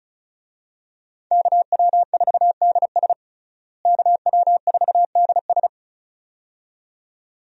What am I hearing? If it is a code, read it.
Morse code, heard as a keyed tone, about 34 wpm, 700 Hz. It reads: KW4DS KW4DS
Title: USA Call Signs - 5 Characters - Repeated 2x - 35wpm